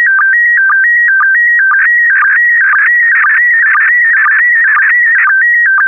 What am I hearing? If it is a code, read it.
SSTV (slow-scan television), PD 120.
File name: ISS-SSTV / SSTV-Audio-ISS-Bild-04